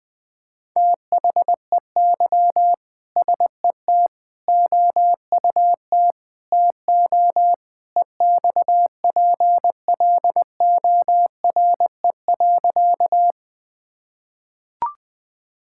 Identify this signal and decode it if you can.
Morse code, heard as a keyed tone, about 20 wpm, 700 Hz. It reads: THEY SET OUT TO EXPLORE.
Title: Sentences with 2-Word Phrases - Spoken Beginning - 20wpm